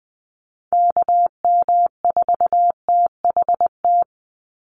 Morse code, heard as a keyed tone, about 20 wpm, 700 Hz. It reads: KM4THT
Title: USA Call Signs 20wpm